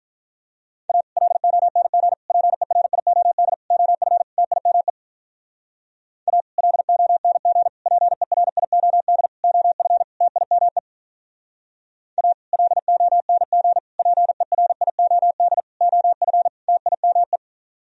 Morse code, heard as a keyed tone, about 44 wpm, 700 Hz. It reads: A LONG PERIOD OF TIME A LONG PERIOD OF TIME A LONG PERIOD OF TIME
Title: Speed-Racing - Sets of 5 Words 35wpm